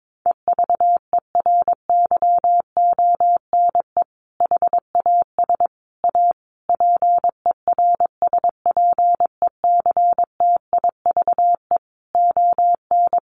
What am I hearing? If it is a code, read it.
Morse code, heard as a keyed tone, about 22 wpm, 700 Hz. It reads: EVERYONE HAS A PERSPECTIVE ON
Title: Sentences from Top 1200 Words 22wpm